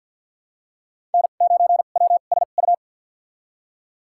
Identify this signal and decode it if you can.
Morse code, heard as a keyed tone, about 52 wpm, spaced as for 36 wpm, 700 Hz. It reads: N9WSU
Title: USA Call Signs - 5 Characters - 2x Character Spacing - 50wpm